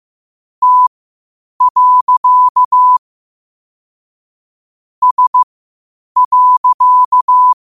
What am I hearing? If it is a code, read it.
Morse code, heard as a keyed tone, about 15 wpm, spaced as for 8 wpm, 1000 Hz. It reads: T. S.
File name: qotdfiles / 8wpm